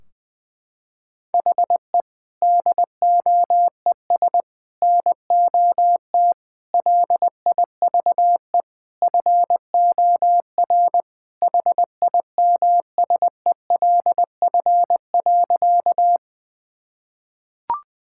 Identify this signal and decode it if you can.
Morse code, heard as a keyed tone, about 19 wpm, 700 Hz. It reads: HE DOES NOT LIVE FOR HIMSELF.
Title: Sentences from Top 1600 Words 20wpm